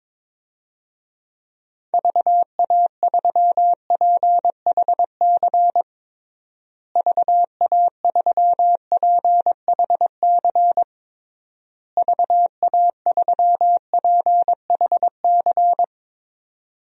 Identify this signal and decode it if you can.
Morse code, heard as a keyed tone, about 22 wpm, 700 Hz. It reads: VA3PHC VA3PHC VA3PHC
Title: DXCC-100 Call Signs - 6 Characters - Repeated 3x - 22wpm